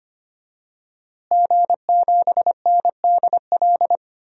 Morse code, heard as a keyed tone, about 25 wpm, 700 Hz. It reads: G7NDL
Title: DXCC-100 Call Signs - 5 Characters - Repeated 3x - 25wpm